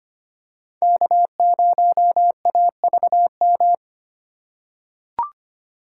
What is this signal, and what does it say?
Morse code, heard as a keyed tone, about 25 wpm, 700 Hz. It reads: K0AVM
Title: USA Call Signs - 5 Characters - 2x Character Spacing - 25wpm